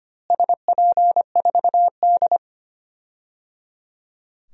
Morse code, heard as a keyed tone, about 25 wpm, 700 Hz. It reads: SP4D
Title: DXCC-100 Call Signs - 4 Characters - Repeated 3x - 25wpm